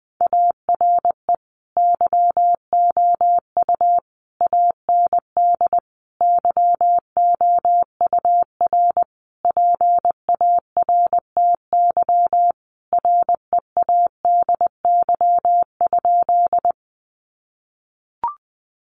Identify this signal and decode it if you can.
Morse code, heard as a keyed tone, about 20 wpm, 700 Hz. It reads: ARE YOU AND YOUR PARTY READY?
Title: Sentences from Top 1400 Words 20wpm